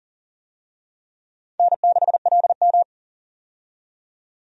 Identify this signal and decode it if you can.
Morse code, heard as a keyed tone, about 40 wpm, 700 Hz. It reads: N6LM
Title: USA Call Signs - 4 Characters - 40wpm